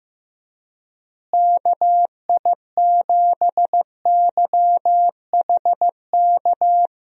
Morse code, heard as a keyed tone, about 16 wpm, 700 Hz. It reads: KI7YHK
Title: USA Call Signs - 6 Characters - Repeated 3x - 15wpm